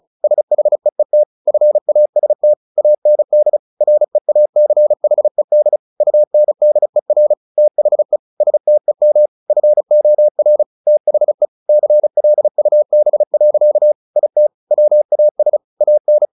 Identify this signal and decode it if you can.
Morse code, heard as a keyed tone, about 34 wpm, 600 Hz. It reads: SHEET FAST AND REACHED UNDER THE STEM FOR THE CLUB. IT WAS AN
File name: the old man and the sea 2 35wpm